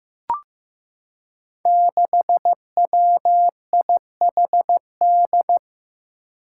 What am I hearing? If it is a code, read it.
Morse code, heard as a keyed tone, about 15 wpm, 700 Hz. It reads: 6WIHD
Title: Five Challenging Letter-Number Combinations 15wpm